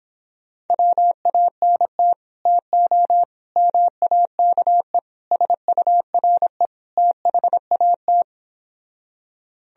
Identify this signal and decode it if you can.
Morse code, heard as a keyed tone, about 26 wpm, 700 Hz. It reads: WANT TO MAKE SURE THAT